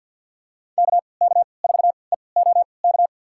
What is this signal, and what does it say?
Morse code, heard as a keyed tone, about 48 wpm, spaced as for 28 wpm, 700 Hz. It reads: KK4EOK
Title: USA Call Signs - 3x Character Spacing - 50wpm